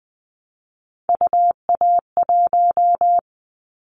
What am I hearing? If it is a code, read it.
Morse code, heard as a keyed tone, about 20 wpm, 700 Hz. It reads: UA1
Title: Callsign Prefixes for Top 100 DXCC - 20wpm